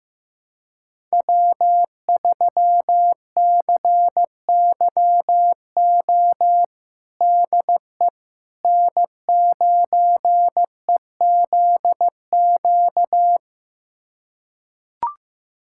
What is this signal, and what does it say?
Morse code, heard as a keyed tone, about 15 wpm, 700 Hz. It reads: W3CYO DE N9EZQ
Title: USA Callsign DE Callsign - 5 Characters - 15wpm